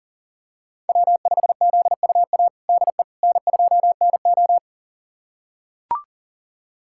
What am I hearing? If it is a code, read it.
Morse code, heard as a keyed tone, about 40 wpm, 700 Hz. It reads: W5ZUA DE N2NO
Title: USA Callsign DE Callsign 40wpm